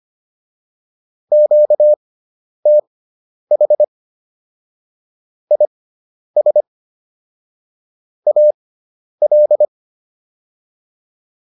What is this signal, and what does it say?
Morse code, heard as a keyed tone, about 25 wpm, spaced as for 10 wpm, 600 Hz. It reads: QTH IS AL